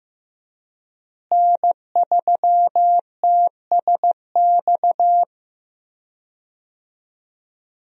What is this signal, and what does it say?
Morse code, heard as a keyed tone, about 15 wpm, 700 Hz. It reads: N3TSX
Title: USA Call Signs - 5 Characters - Repeated 2x - 15wpm